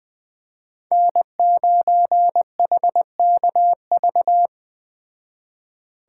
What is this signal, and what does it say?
Morse code, heard as a keyed tone, about 20 wpm, 700 Hz. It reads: N9HKV
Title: Speed Racing - USA 1x3 Call Signs 17wpm